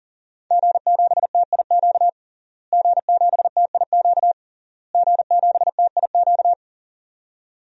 Morse code, heard as a keyed tone, about 41 wpm, 700 Hz. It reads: G7TIQ G7TIQ G7TIQ
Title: DXCC-100 Call Signs - 5 Characters - Repeated 3x - 40wpm